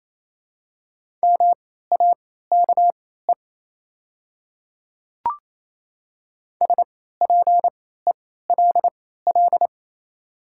Morse code, heard as a keyed tone, about 27 wpm, spaced as for 16 wpm, 700 Hz. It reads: MAKE SPELL
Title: 3x Character Spacing - Top 300 Words - 28wpm